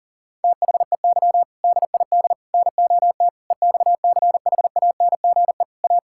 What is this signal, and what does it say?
Morse code, heard as a keyed tone, about 40 wpm, 700 Hz. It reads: THEY DID NOT EXCHANGE A
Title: Sentences from Top 1000 Words - Review Part 6 - 40wpm